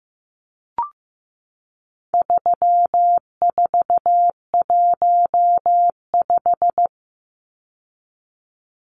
Morse code, heard as a keyed tone, about 15 wpm, 700 Hz. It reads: 3415
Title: Numbers - 4-Digit 15wpm